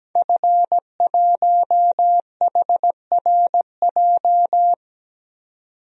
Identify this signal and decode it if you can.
Morse code, heard as a keyed tone, about 17 wpm, 700 Hz. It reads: F1HRJ
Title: DXCC-100 Call Signs - 5 Characters - 17wpm